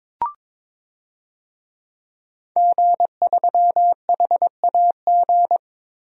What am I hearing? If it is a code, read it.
Morse code, heard as a keyed tone, about 23 wpm, 700 Hz. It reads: G3HAG
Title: DXCC-100 Call Signs - 5 Characters - 22wpm